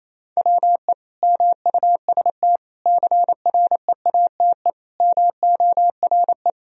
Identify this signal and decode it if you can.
Morse code, heard as a keyed tone, about 28 wpm, 700 Hz. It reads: WE MUST CREATE MORE
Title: Sentences from Top 1000 Words - Review Part 3 - 28wpm